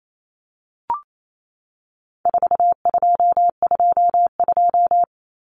Morse code, heard as a keyed tone, about 28 wpm, 700 Hz. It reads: 4222
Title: Common Cut Numbers - 4-Digit 28wpm